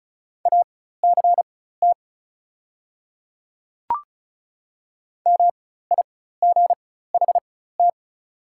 Morse code, heard as a keyed tone, about 34 wpm, spaced as for 16 wpm, 700 Hz. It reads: ACT MIGHT
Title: 4x Character Spacing - Top 300 Words - 35wpm